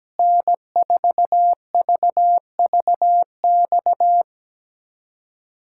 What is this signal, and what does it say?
Morse code, heard as a keyed tone, about 17 wpm, 700 Hz. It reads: N4VVX
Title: USA Callsign DE Callsign 17wpm